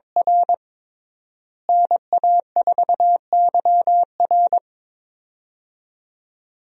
Morse code, heard as a keyed tone, about 22 wpm, 700 Hz. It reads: R NA4YR
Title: USA Call Signs - 5 Characters - Repeated 3x - 22wpm